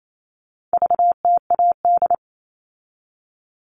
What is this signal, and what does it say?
Morse code, heard as a keyed tone, about 28 wpm, 700 Hz. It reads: VTAD
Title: All Cut Numbers - 4-Digit 28wpm